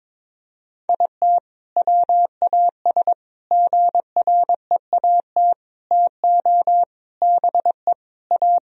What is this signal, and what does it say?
Morse code, heard as a keyed tone, about 22 wpm, 700 Hz. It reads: IT WAS GREAT TO BE A
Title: Sentences from Top 1000 Words - Review Part 2 - 22wpm